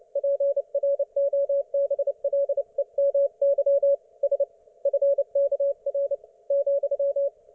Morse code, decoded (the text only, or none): PROBLEMY S FKR ,